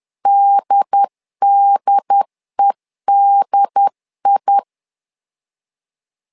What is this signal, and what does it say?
Morse code, heard as a keyed tone, about 10 wpm, 800 Hz. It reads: DDEDI
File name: LEZIONI / 2 UHDN / 2 UHDN 10wpm 1904